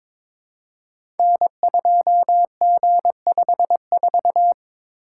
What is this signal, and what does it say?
Morse code, heard as a keyed tone, about 22 wpm, 700 Hz. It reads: N2G54
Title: Code Groups - 5 Letter:Numbers - 22wpm